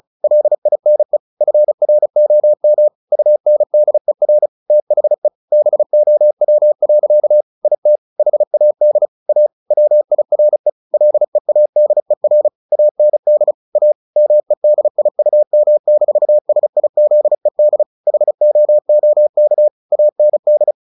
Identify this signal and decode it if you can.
Morse code, heard as a keyed tone, about 34 wpm, 600 Hz. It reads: LINE FROM UNDER THE BOW. IT HAD A WIRE LEADER AND A MEDIUM-SIZED HOOK AND
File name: the old man and the sea / the old man and the sea 1 35wpm